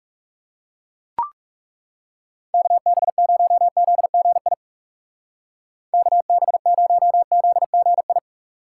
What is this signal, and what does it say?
Morse code, heard as a keyed tone, about 35 wpm, 700 Hz. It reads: KB0ZGI KB0ZGI
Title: Speed Racing - USA 2x3 Call Signs 30wpm